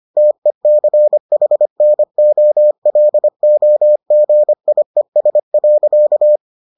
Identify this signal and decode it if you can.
Morse code, heard as a keyed tone, about 25 wpm, 600 Hz. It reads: TECHNOLOGIES.